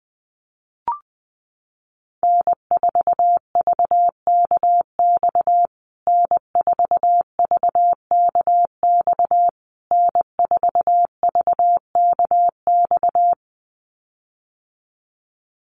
Morse code, heard as a keyed tone, about 20 wpm, 700 Hz. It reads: N4VKX N4VKX N4VKX
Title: USA Call Signs Repeated 3x 20wpm